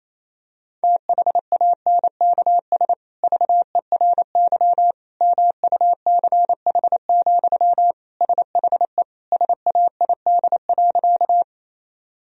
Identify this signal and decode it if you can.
Morse code, heard as a keyed tone, about 27 wpm, 700 Hz. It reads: THANKS VERY MUCH, SHE SAID.
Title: Sentences from Top 1000 Words - Review Part 3 - 28wpm